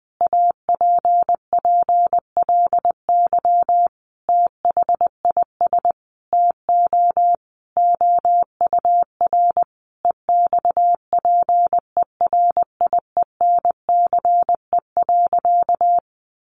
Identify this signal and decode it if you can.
Morse code, heard as a keyed tone, about 20 wpm, 700 Hz. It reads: APPLY THIS TO OUR EXPERIENCE.
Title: Sentences from Top 1200 Words 20wpm